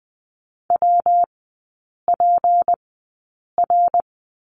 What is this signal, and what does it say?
Morse code, heard as a keyed tone, about 20 wpm, spaced as for 8 wpm, 700 Hz. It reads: WPR
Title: 191 Lesson 31e - Letter v - Three Characters 20wpm